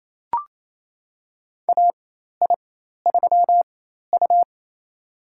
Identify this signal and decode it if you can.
Morse code, heard as a keyed tone, about 27 wpm, spaced as for 13 wpm, 700 Hz. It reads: AI3U
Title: USA Call Signs - 4 Characters - 4x Character Spacing - 28wpm